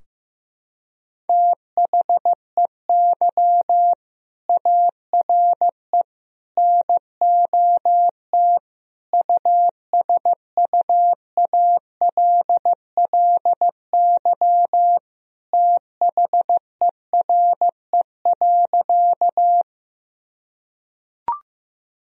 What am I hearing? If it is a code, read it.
Morse code, heard as a keyed tone, about 15 wpm, 700 Hz. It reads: THEY ARE NOT USUALLY THERE.
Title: Sentences from Top 1000 Words - Review Part 3 - 15wpm